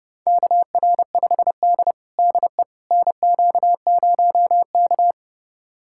Morse code, heard as a keyed tone, about 29 wpm, 700 Hz. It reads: KR5D DE NQ0K
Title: USA Callsign DE Callsign - 4 Characters - 30wpm